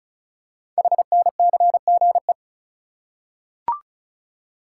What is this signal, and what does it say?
Morse code, heard as a keyed tone, about 36 wpm, 700 Hz. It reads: HNCGE
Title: Code Groups - 5 Letters - 35wpm